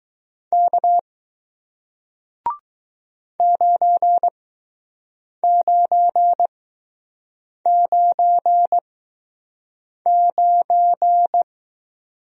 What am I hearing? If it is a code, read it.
Morse code, heard as a keyed tone, about 20 wpm, spaced as for 6 wpm, 700 Hz. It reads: K 9999